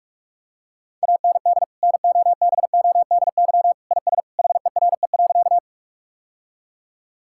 Morse code, heard as a keyed tone, about 46 wpm, 700 Hz. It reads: AND NOBODY IS HERE.
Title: Sentences from Top 1600 Words 45wpm